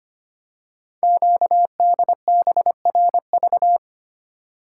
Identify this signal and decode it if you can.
Morse code, heard as a keyed tone, about 25 wpm, 700 Hz. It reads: QDBRV
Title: Code Groups - 5 Letters - 25wpm